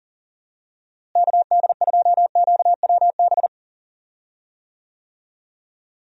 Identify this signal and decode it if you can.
Morse code, heard as a keyed tone, about 39 wpm, 700 Hz. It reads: KD2QWB
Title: USA Call Signs - 6 Characters - Repeated 3x - 40wpm